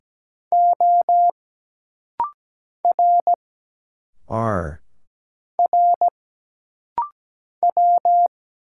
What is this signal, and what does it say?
Morse code, heard as a keyed tone, about 17 wpm, 700 Hz. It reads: O R R W
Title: Single Letters - Rapid-Fire 17wpm